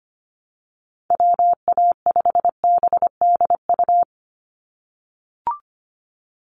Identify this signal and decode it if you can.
Morse code, heard as a keyed tone, about 25 wpm, 700 Hz. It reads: WA5BDU
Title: Call Signs - 25wpm